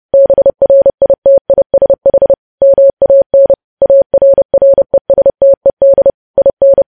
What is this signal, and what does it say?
Morse code, heard as a keyed tone, about 30 wpm, 550 Hz. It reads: BRITISH MAN ARRESTED IN
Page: news headlines in morse code at 30 wpm